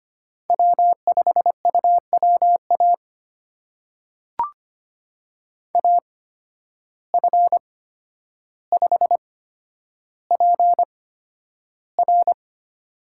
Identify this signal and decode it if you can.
Morse code, heard as a keyed tone, about 25 wpm, spaced as for 11 wpm, 700 Hz. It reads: W5UWA AF5PR
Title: USA Call Signs - 5 Characters - 8x Character Spacing - 25wpm